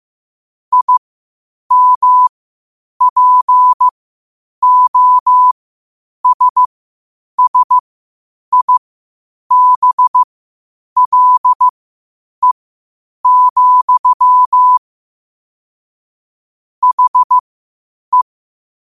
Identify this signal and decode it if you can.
Morse code, heard as a keyed tone, about 15 wpm, spaced as for 8 wpm, 1000 Hz. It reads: IMPOSSIBLE, HE
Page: New quotes every day in morse code at 8 Words per minute.